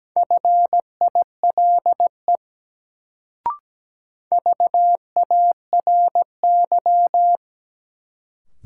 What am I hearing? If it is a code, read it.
Morse code, heard as a keyed tone, about 17 wpm, 700 Hz. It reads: FILE VARY
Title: Sentences from Top 1200 Words - New Word Drill Alt - 17wpm